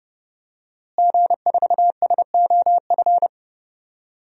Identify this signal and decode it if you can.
Morse code, heard as a keyed tone, about 30 wpm, 700 Hz. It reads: G4SOF
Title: DXCC-100 Call Signs - 5 Characters - Repeated 2x - 30wpm